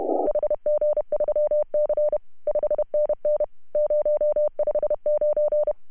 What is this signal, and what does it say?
Morse code, heard as a keyed tone, about 31 wpm, 590 Hz. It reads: HG3C 5NN 059
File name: CW / HA2OS RHOB 2017 CW